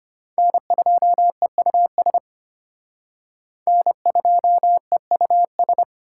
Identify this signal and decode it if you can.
Morse code, heard as a keyed tone, about 33 wpm, 700 Hz. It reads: N2EUS N2EUS